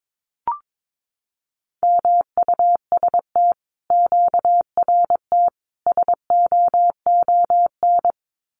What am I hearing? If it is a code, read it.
Morse code, heard as a keyed tone, about 22 wpm, 700 Hz. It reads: MUST QRT SOON
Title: Common QSO Phrases Encore 22wpm